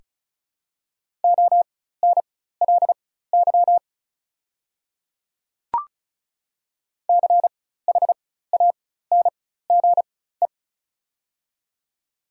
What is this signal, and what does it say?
Morse code, heard as a keyed tone, about 36 wpm, spaced as for 16 wpm, 700 Hz. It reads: ONLY CHANGE
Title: 4x Character Spacing - Top 300 Words - 35wpm